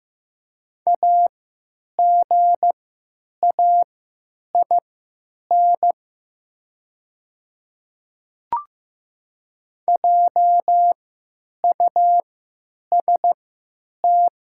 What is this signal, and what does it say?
Morse code, heard as a keyed tone, about 15 wpm, spaced as for 9 wpm, 700 Hz. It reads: AGAIN JUST
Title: 3x Character Spacing - Top 200 Words - 15wpm